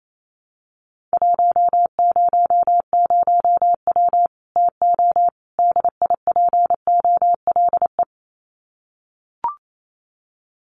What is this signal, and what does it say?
Morse code, heard as a keyed tone, about 28 wpm, 700 Hz. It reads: 100W TO DIPOLE
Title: Common QSO Phrases 28wpm